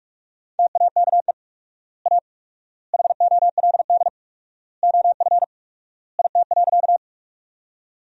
Morse code, heard as a keyed tone, about 46 wpm, 700 Hz. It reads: TAKE A HOLD OF IT.
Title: 4x Word-Spacing - Sentences from Top 400 Words - 45wpm